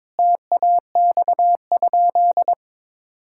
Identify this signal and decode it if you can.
Morse code, heard as a keyed tone, about 22 wpm, 700 Hz. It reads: TAX?
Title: Sentences from Top 1000 Words - Review Part 5 - 22wpm